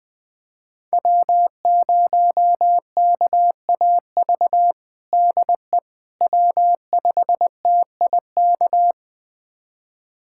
Morse code, heard as a keyed tone, about 20 wpm, 700 Hz. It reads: W0KAV DE W5TIK
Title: USA Callsign DE Callsign - 5 Characters - 20wpm